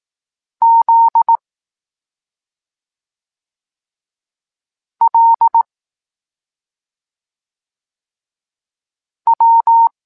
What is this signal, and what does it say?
Morse code, heard as a keyed tone, about 18 wpm, 920 Hz. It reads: Z L W
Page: Tekst 23.3 – Ciąg dalszy doskonalenia odbioru poznanych liter